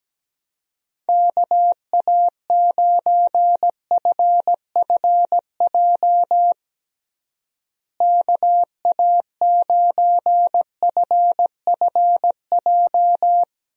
Morse code, heard as a keyed tone, about 17 wpm, 700 Hz. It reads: KA9FFJ KA9FFJ
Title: USA Call Signs - 6 Characters - Repeated 3x - 17wpm